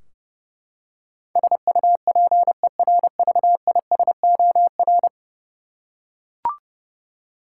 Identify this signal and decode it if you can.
Morse code, heard as a keyed tone, about 30 wpm, 700 Hz. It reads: SUPERVISOR
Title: Practice QSO - Exchange 3 - More Info - 30wpm